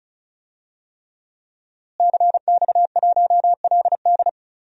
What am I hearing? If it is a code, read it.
Morse code, heard as a keyed tone, about 34 wpm, 700 Hz. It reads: CX1LD
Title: DXCC-100 Call Signs - 5 Characters - Repeated 3x - 35wpm